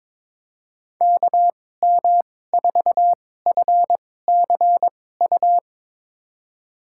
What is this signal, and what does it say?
Morse code, heard as a keyed tone, about 22 wpm, spaced as for 16 wpm, 700 Hz. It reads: KM4FCU
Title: USA Call Signs - 6 Characters - 2x Character Spacing - 22wpm